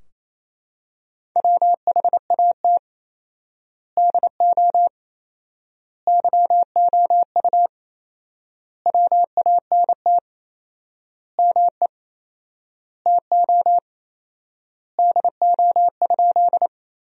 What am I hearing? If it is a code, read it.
Morse code, heard as a keyed tone, about 28 wpm, 700 Hz. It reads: WHAT DO YOU WANT ME TO DO?
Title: 4x Word-Spacing - Sentences from Top 200 Words - 28wpm